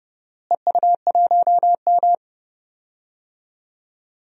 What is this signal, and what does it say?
Morse code, heard as a keyed tone, about 29 wpm, 700 Hz. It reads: EU1M
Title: DXCC-100 Call Signs - 4 Characters - Repeated 2x - 30wpm